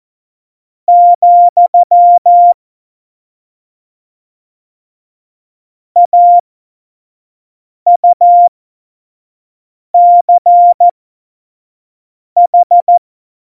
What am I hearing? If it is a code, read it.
Morse code, heard as a keyed tone, about 14 wpm, spaced as for 5 wpm, 700 Hz. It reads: , AUCH